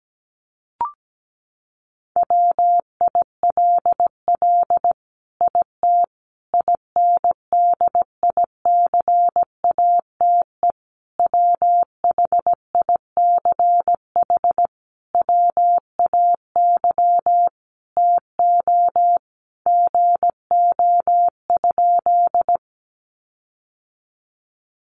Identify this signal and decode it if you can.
Morse code, heard as a keyed tone, about 17 wpm, 700 Hz. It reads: WILL IT INDICATE WHICH WAY TO GO?
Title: Sentences from Top 1300 Words 17wpm